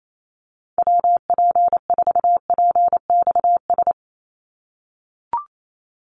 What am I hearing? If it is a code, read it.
Morse code, heard as a keyed tone, about 28 wpm, 700 Hz. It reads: WP4PXS
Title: USA Call Signs 28wpm